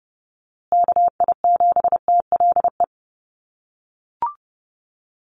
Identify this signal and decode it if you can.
Morse code, heard as a keyed tone, about 30 wpm, 700 Hz. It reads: KI7TLE
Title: USA Call Signs Repeated 2x 30wpm